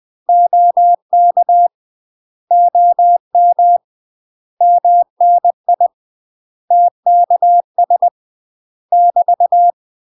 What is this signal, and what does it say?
Morse code, heard as a keyed tone, about 20 wpm, 700 Hz. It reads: OK OM MNI TKS <BT>
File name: UK MASTER QSOs TRAINING / 20WPM / UK MASTER QSOs 0052